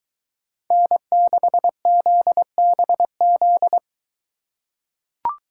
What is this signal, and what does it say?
Morse code, heard as a keyed tone, about 21 wpm, 700 Hz. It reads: N6ZBZ